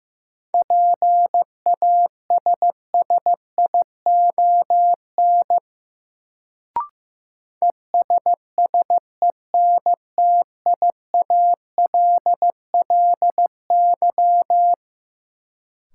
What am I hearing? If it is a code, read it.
Morse code, heard as a keyed tone, about 15 wpm, 700 Hz. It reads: PASSION ESSENTIALLY
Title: Sentences from Top 1300 Words - New Word Drill - 15wpm